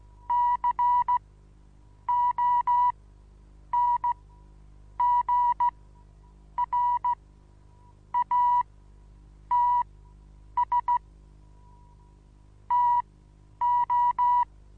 Morse code, heard as a keyed tone, about 15 wpm, spaced as for 8 wpm, 970 Hz. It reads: CONGRATS TO
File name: Onlinelehrgang / morsen / DJ4UF Morsen25